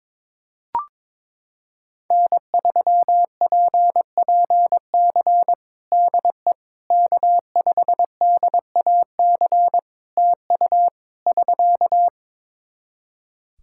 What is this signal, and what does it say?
Morse code, heard as a keyed tone, about 22 wpm, 700 Hz. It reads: N3PPC DE K5DAC TU <SK>
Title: K1USN SST QSO Practice 22wpm